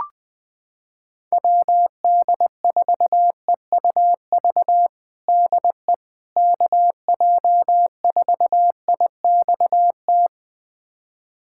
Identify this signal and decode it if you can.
Morse code, heard as a keyed tone, about 21 wpm, 700 Hz. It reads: WD4EUV DE KJ4IXT
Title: USA Callsign DE Callsign - 6 Characters - 20wpm